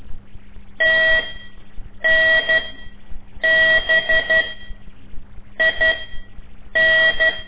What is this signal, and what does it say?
Morse code, heard as a keyed tone, about 12 wpm, spaced as for 7 wpm, 1950 Hz. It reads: TNBIN
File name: ik1weg2 / CORSI CW / lez5-1